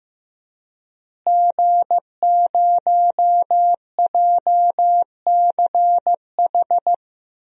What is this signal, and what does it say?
Morse code, heard as a keyed tone, about 15 wpm, 700 Hz. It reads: G0JCH
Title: DXCC-100 Call Signs - 5 Characters - Repeated 2x - 15wpm